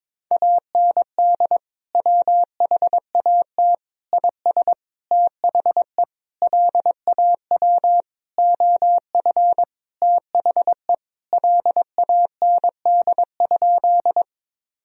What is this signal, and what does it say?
Morse code, heard as a keyed tone, about 22 wpm, 700 Hz. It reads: AND WHAT IS THE LAW OF THE LAND?
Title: Sentences from Top 1000 Words - Review Part 3 - 22wpm